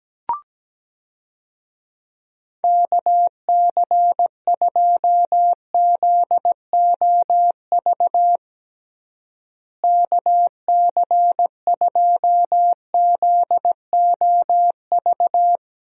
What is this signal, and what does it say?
Morse code, heard as a keyed tone, about 17 wpm, 700 Hz. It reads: KC2ZOV KC2ZOV
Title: USA Call Signs - 6 Characters - Repeated 2x - 17wpm